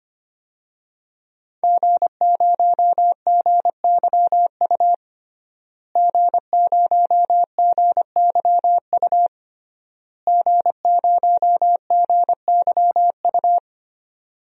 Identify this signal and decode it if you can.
Morse code, heard as a keyed tone, about 25 wpm, 700 Hz. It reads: G0GYU G0GYU G0GYU
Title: DXCC-100 Call Signs - 5 Characters - Repeated 3x - 25wpm